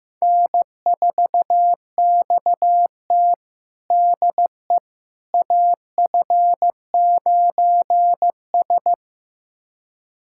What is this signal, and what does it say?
Morse code, heard as a keyed tone, about 15 wpm, 700 Hz. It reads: N4XT DE AF9S
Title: USA Callsign DE Callsign - 4 Characters - 15wpm